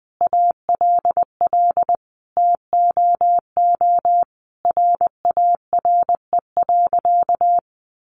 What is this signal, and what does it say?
Morse code, heard as a keyed tone, about 20 wpm, 700 Hz. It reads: ALL TOO RARE.
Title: Sentences from Top 1500 Words 20wpm